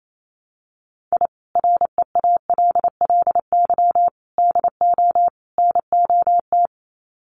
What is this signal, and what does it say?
Morse code, heard as a keyed tone, about 28 wpm, 700 Hz. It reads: I REALLY DO NOT
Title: Sets of 4 Words 28wpm